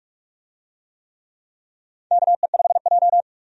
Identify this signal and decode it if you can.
Morse code, heard as a keyed tone, about 46 wpm, 700 Hz. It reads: KE5J
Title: USA Call Signs - 4 Characters - 45wpm